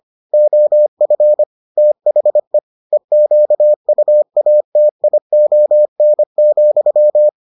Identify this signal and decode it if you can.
Morse code, heard as a keyed tone, about 25 wpm, 600 Hz. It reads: OF THE EQUATION,